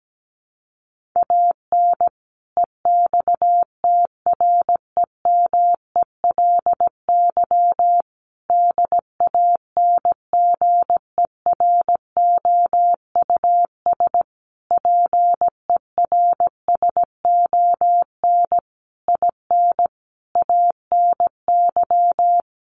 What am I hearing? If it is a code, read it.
Morse code, heard as a keyed tone, about 17 wpm, 700 Hz. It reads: AN EXTREMELY DANGEROUS PERSON IN ANY
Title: Sentences from Top 1200 Words 17wpm